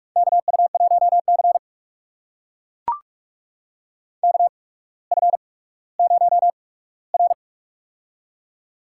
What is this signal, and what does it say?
Morse code, heard as a keyed tone, about 44 wpm, spaced as for 19 wpm, 700 Hz. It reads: KU1C KF0R
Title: USA Call Signs - 4 Characters - 8x Character Spacing - 45wpm